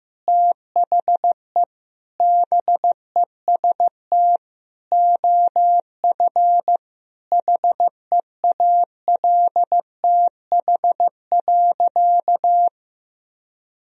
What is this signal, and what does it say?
Morse code, heard as a keyed tone, about 16 wpm, 700 Hz. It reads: THE BEST OF HEALTH.
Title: Sentences from Top 1000 Words - Review Part 3 - 15wpm